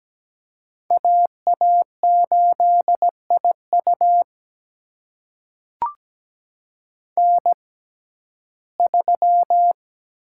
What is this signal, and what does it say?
Morse code, heard as a keyed tone, about 17 wpm, spaced as for 12 wpm, 700 Hz. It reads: AA8IU N3
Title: USA Call Signs - 5 Characters - 6x Character Spacing - 17wpm